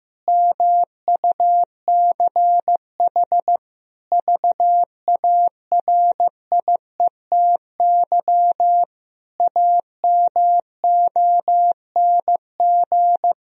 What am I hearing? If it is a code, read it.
Morse code, heard as a keyed tone, about 16 wpm, 700 Hz. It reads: MUCH VARIETY AMONG
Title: Sentences from Top 1000 Words - Review Part 4 - 15wpm